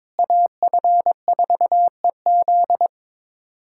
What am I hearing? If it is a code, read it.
Morse code, heard as a keyed tone, about 22 wpm, 700 Hz. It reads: AF4EZ
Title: USA Callsign DE Callsign - 5 Characters - 22wpm